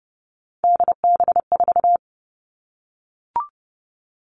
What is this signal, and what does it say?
Morse code, heard as a keyed tone, about 30 wpm, 700 Hz. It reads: DB4
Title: All Cut Numbers - 3-Digit 30wpm